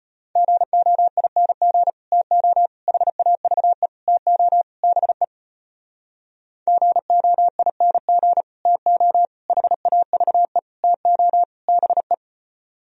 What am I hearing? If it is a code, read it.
Morse code, heard as a keyed tone, about 33 wpm, 700 Hz. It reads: GOING TO HAVE TO BE GOING TO HAVE TO BE